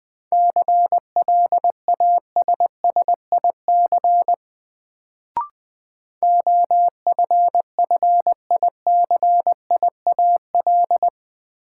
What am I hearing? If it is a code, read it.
Morse code, heard as a keyed tone, about 20 wpm, 700 Hz. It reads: CLASSIC OFFICIAL
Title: Sentences from Top 1000 Words - New Word Drill - 20wpm